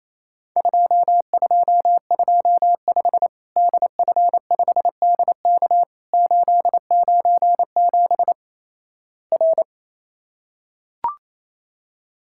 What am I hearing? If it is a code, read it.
Morse code, heard as a keyed tone, about 28 wpm, 700 Hz. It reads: 2225 DF5DK 897 R
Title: Worked All Europe DX Contest - QTC Exchange - 28wpm